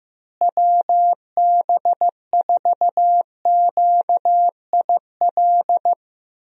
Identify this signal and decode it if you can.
Morse code, heard as a keyed tone, about 15 wpm, 700 Hz. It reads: WB4QIL
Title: USA Call Signs - 6 Characters - Repeated 3x - 15wpm